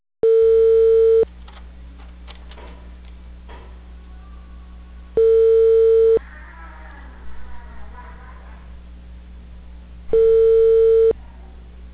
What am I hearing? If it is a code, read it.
Ringback tone (China, 450 Hz).